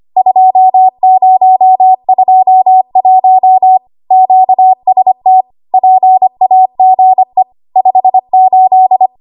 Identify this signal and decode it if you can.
Morse code, heard as a keyed tone, about 25 wpm, 750 Hz. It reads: 2021 QST PAGE 58
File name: Archive / 25 WPM / 25WPM